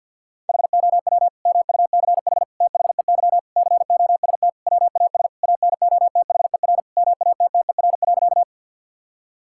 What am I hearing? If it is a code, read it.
Morse code, heard as a keyed tone, about 51 wpm, 700 Hz. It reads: HOW MUCH THEY COST WAS ANOTHER MATTER.
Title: Sentences from Top 1000 Words - Review Part 5 - 50wpm